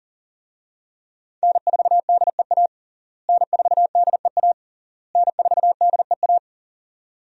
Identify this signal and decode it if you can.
Morse code, heard as a keyed tone, about 40 wpm, 700 Hz. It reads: N4DEA N4DEA N4DEA
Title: USA Call Signs - 5 Characters - Repeated 3x - 40wpm